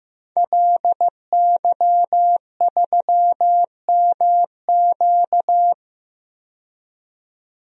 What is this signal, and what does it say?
Morse code, heard as a keyed tone, about 15 wpm, 700 Hz. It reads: LY3MQ
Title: DXCC-100 Call Signs - 5 Characters - 15wpm